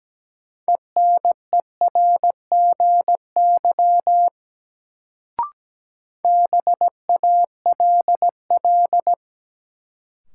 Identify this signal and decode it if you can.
Morse code, heard as a keyed tone, about 17 wpm, 700 Hz. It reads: ENERGY BALL
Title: Sentences from Top 900 Words - New Word Drill - 17wpm